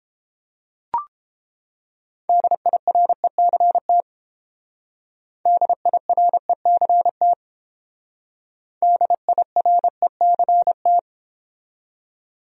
Morse code, heard as a keyed tone, about 28 wpm, 700 Hz. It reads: DIRECT DIRECT DIRECT